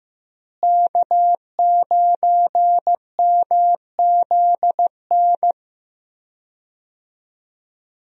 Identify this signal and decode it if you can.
Morse code, heard as a keyed tone, about 15 wpm, 700 Hz. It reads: K9MZN
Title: USA Call Signs - 5 Characters - Repeated 3x - 15wpm